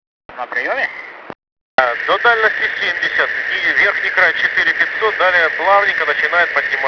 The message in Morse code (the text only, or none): none